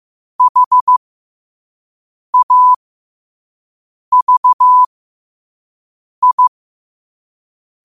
Morse code, heard as a keyed tone, about 15 wpm, spaced as for 5 wpm, 1000 Hz. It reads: HAVI
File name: qotdfiles / 5wpm